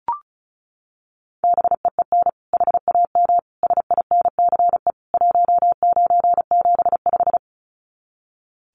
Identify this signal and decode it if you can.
Morse code, heard as a keyed tone, about 34 wpm, 700 Hz. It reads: BEEN HAM SINCE 1975
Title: Common QSO Phrases 35wpm